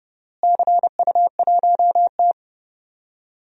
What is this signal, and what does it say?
Morse code, heard as a keyed tone, about 30 wpm, 700 Hz. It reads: CU1T
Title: DXCC-100 Call Signs - 4 Characters - Repeated 3x - 30wpm